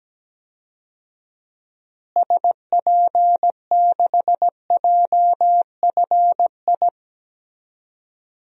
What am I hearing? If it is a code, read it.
Morse code, heard as a keyed tone, about 17 wpm, 700 Hz. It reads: SP6JFI
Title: DXCC-100 Call Signs - 6 Characters - Repeated 2x - 17wpm